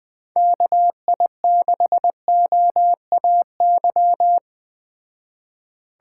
Morse code, heard as a keyed tone, about 20 wpm, 700 Hz. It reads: KI6OAY